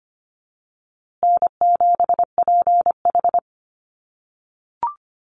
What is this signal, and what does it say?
Morse code, heard as a keyed tone, about 25 wpm, 700 Hz. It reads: N7PH
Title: USA Call Signs Repeated 3x 25wpm